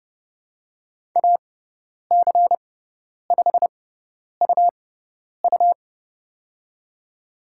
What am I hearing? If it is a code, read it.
Morse code, heard as a keyed tone, about 31 wpm, spaced as for 10 wpm, 700 Hz. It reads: AC5UU
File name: Lesson 18g - Letter m - Call Signs 30wpm